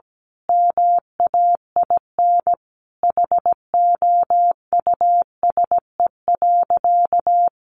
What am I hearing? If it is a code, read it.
Morse code, heard as a keyed tone, about 17 wpm, 700 Hz. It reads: MAIN HOUSE.
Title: Sentences from Top 1300 Words 17wpm